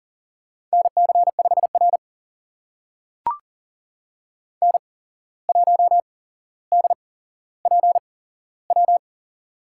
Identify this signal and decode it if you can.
Morse code, heard as a keyed tone, about 40 wpm, spaced as for 16 wpm, 700 Hz. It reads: NC5R N1DPW
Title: USA Call Signs - 8x Character Spacing - 40wpm